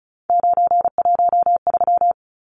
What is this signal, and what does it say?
Morse code, heard as a keyed tone, about 34 wpm, 700 Hz. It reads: 913
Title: Numbers - 3-Digit 35wpm